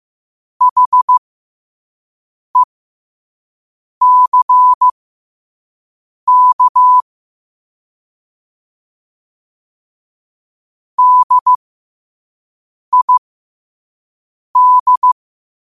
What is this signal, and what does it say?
Morse code, heard as a keyed tone, about 15 wpm, spaced as for 5 wpm, 1000 Hz. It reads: HECK DID
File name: qotdfiles / 5wpm